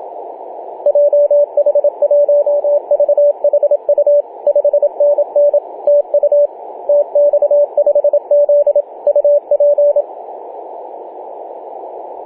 Morse code, heard as a keyed tone, about 27 wpm, 580 Hz, with radio noise. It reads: JH1VHU 5NN TU TX5Z UP
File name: posts / oct-08-2033z-tx5z-21-008mhz-cw